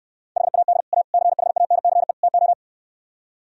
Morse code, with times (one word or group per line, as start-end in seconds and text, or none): 0.4-0.8 HAD
0.9-1.0 A
1.1-2.1 CHANCE
2.2-2.5 TO